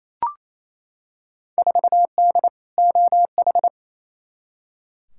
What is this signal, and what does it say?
Morse code, heard as a keyed tone, about 28 wpm, 700 Hz. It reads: 4D OH
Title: Field Day Exchanges Encore - 28wpm